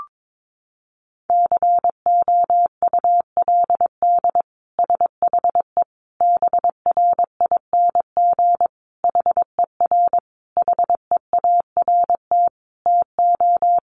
Morse code, heard as a keyed tone, about 22 wpm, 700 Hz. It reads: COULD SHE BRING HER HEART TO
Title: Sentences from Top 1400 Words 22wpm